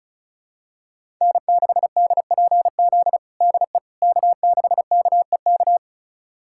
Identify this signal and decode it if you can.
Morse code, heard as a keyed tone, about 34 wpm, 700 Hz. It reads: N6DPZ DE K6KEK
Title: USA Callsign DE Callsign - 5 Characters - 35wpm